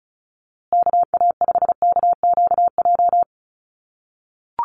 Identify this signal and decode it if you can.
Morse code, heard as a keyed tone, about 34 wpm, 700 Hz. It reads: KA5KQJ
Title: USA Call Signs Repeated 2x 35wpm